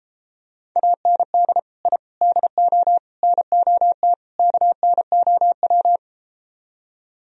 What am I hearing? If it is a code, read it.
Morse code, heard as a keyed tone, about 33 wpm, 700 Hz. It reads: AND I DO NOT KNOW